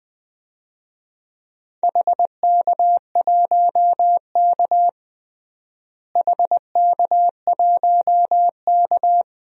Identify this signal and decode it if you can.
Morse code, heard as a keyed tone, about 21 wpm, 700 Hz. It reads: HK1K HK1K
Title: DXCC-100 Call Signs - 4 Characters - Repeated 3x - 20wpm